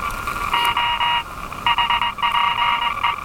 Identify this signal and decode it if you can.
Morse code, heard as a keyed tone, about 21 wpm, 950 Hz, with noise. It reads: O HPE
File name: attachments / RV3GM EW1CY